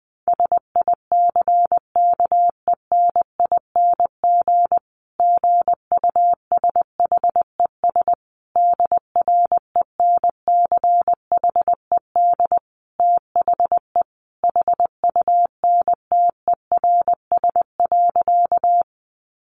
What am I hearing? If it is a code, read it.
Morse code, heard as a keyed tone, about 19 wpm, 700 Hz. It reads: SICKENING GUSHES DRENCHED THE HUNTERS.
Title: A Sound Of Thunder 20wpm